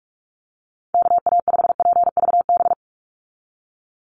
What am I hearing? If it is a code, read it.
Morse code, heard as a keyed tone, about 44 wpm, 700 Hz. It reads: KA5PVB
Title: NA SOTA Chaser Callsigns 45wpm